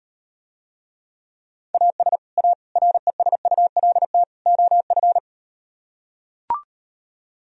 Morse code, heard as a keyed tone, about 38 wpm, 700 Hz. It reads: AS A RESULT OF